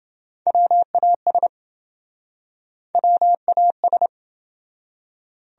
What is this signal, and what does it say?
Morse code, heard as a keyed tone, about 27 wpm, 700 Hz. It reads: WAS WAS